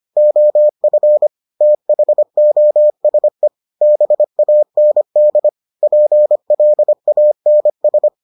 Morse code, heard as a keyed tone, about 25 wpm, 600 Hz. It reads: OF THOSE BAND PLANS